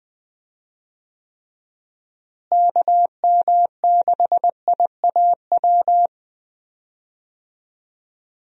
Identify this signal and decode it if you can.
Morse code, heard as a keyed tone, about 20 wpm, 700 Hz. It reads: KM6IAW
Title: USA Call Signs - 6 Characters - Repeated 2x - 20wpm